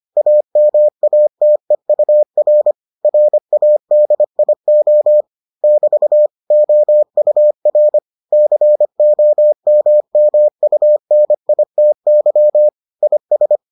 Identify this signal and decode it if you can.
Morse code, heard as a keyed tone, about 25 wpm, 600 Hz. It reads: AMATEUR RADIO <BT> OUR COMMUNITY IS